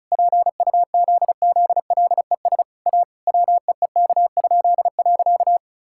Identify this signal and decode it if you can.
Morse code, heard as a keyed tone, about 36 wpm, 700 Hz. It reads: PUZZLES A WEEK?.
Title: Pangram Challenge v2 - 35wpm